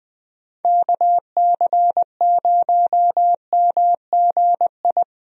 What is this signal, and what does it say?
Morse code, heard as a keyed tone, about 21 wpm, 700 Hz. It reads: KC0MGI
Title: USA Call Signs - 6 Characters - Repeated 3x - 20wpm